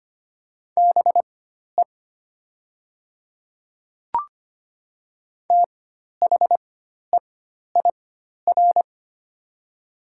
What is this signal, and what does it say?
Morse code, heard as a keyed tone, about 25 wpm, spaced as for 12 wpm, 690 Hz. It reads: BE THEIR
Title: 4x Character Spacing - Top 100 Words - 25wpm